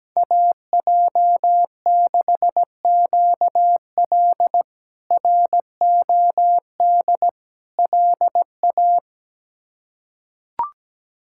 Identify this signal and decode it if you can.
Morse code, heard as a keyed tone, about 17 wpm, 700 Hz. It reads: AJ6QL ROD LA
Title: K1USN SST QSO Practice - 17wpm